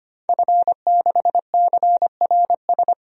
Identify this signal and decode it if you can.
Morse code, heard as a keyed tone, about 25 wpm, 700 Hz. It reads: F6CRS
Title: DXCC-100 Call Signs - 5 Characters - 25wpm